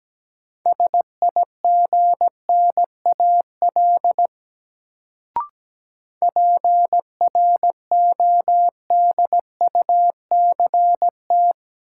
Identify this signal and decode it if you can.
Morse code, heard as a keyed tone, about 17 wpm, 700 Hz. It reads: SIGNAL PRODUCT
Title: Sentences from Top 1400 Words - New Word Drill Alt - 17wpm